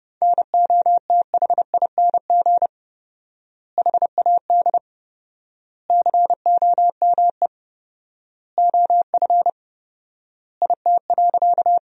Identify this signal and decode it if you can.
Morse code, heard as a keyed tone, about 30 wpm, 700 Hz. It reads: NOTHING HAD COME OF IT.
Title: 4x Word-Spacing - Sentences from Top 500 Words - 30wpm